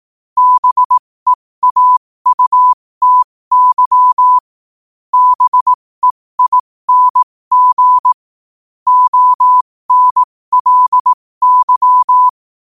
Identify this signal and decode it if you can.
Morse code, heard as a keyed tone, about 18 wpm, spaced as for 15 wpm, 1000 Hz. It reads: BEAUTY BEING ONLY
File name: qotdfiles / 15wpm